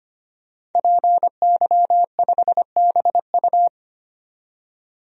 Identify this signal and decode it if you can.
Morse code, heard as a keyed tone, about 25 wpm, 700 Hz. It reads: PY5BU
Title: DXCC-100 Call Signs - 5 Characters - 25wpm